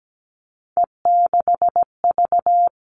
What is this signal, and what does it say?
Morse code, heard as a keyed tone, about 17 wpm, 700 Hz. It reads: E6V
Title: All Cut Numbers - 3-Digit 17wpm